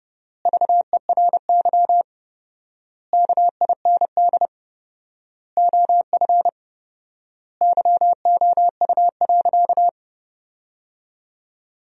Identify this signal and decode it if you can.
Morse code, heard as a keyed tone, about 30 wpm, 700 Hz. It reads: VERY KIND OF YOU.
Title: 4x Word-Spacing - Sentences from Top 200 Words - 30wpm